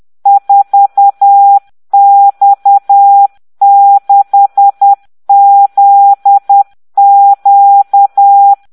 Morse code, heard as a keyed tone, about 10 wpm, 800 Hz. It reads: 4X6ZQ